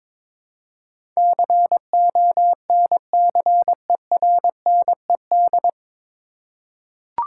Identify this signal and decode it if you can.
Morse code, heard as a keyed tone, about 22 wpm, 700 Hz. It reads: CONCERNED
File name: 9 Letter Words - First 3 Letters Given - 22wpm